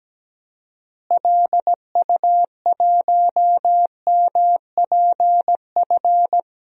Morse code, heard as a keyed tone, about 17 wpm, 700 Hz. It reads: LU1MPF
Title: DXCC-100 Call Signs - 6 Characters - Repeated 2x - 17wpm